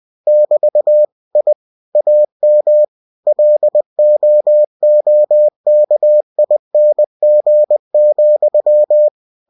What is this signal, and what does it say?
Morse code, heard as a keyed tone, about 20 wpm, 600 Hz. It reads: <BT> I AM LOOKING,
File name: cw-books / winnie-the-pooh / winnie-20wpm-page-0072